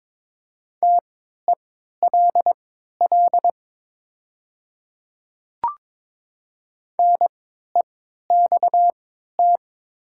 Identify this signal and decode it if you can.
Morse code, heard as a keyed tone, about 22 wpm, spaced as for 13 wpm, 700 Hz. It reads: TELL NEXT
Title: 3x Character Spacing - Top 500 Words - 22wpm